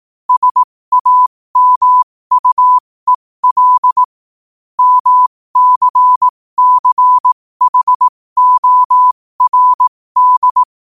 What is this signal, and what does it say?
Morse code, heard as a keyed tone, about 18 wpm, spaced as for 15 wpm, 1000 Hz. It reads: SAMUEL MCCHORD